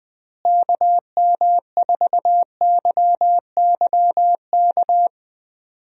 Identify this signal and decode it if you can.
Morse code, heard as a keyed tone, about 21 wpm, 700 Hz. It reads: KM4YYK
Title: USA Call Signs - 6 Characters - Repeated 2x - 20wpm